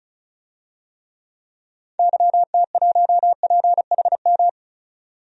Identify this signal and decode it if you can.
Morse code, heard as a keyed tone, about 34 wpm, 700 Hz. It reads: YT1PHM
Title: DXCC-100 Call Signs - 6 Characters - 35wpm